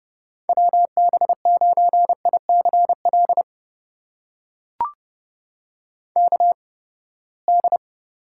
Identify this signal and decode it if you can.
Morse code, heard as a keyed tone, about 31 wpm, spaced as for 21 wpm, 700 Hz. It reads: WB9ICL KD
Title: USA Call Signs - 6 Characters - 8x Character Spacing - 30wpm